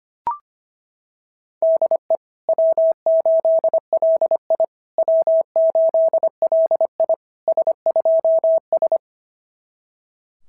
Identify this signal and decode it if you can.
Morse code, heard as a keyed tone, about 25 wpm, 660 Hz. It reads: DE W8LI W8LI S2S
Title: SOTA QSO Practice 25wpm